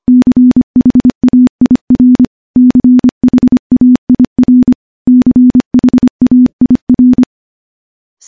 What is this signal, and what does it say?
Morse code, heard as a keyed tone, about 25 wpm, 250 Hz. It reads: CHAIR CHAIR CHAIR
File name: morse-training-25WPM